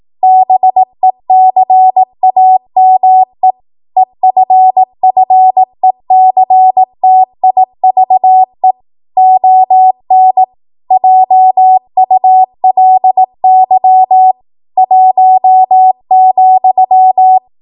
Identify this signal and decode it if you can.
Morse code, heard as a keyed tone, about 18 wpm, 750 Hz. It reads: BECAME EFFECTIVE ON JULY 1,